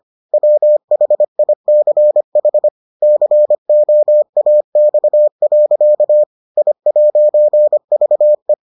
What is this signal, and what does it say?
Morse code, heard as a keyed tone, about 25 wpm, 600 Hz. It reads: WHICH COAX. I'VE